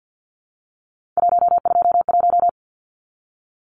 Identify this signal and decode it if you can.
Morse code, heard as a keyed tone, about 50 wpm, 700 Hz. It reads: 121
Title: Numbers - 3-Digit 50wpm